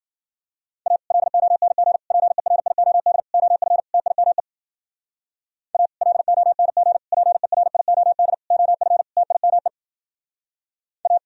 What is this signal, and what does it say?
Morse code, heard as a keyed tone, about 50 wpm, 700 Hz. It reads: A LONG PERIOD OF TIME A LONG PERIOD OF TIME A